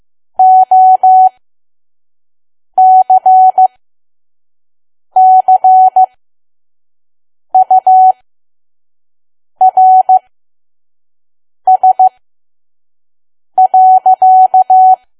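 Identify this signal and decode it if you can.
Morse code, heard as a keyed tone, about 15 wpm, spaced as for 5 wpm, 750 Hz. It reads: OCCURS.
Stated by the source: Morse code practice at 5 WPM from KQ2P.COM.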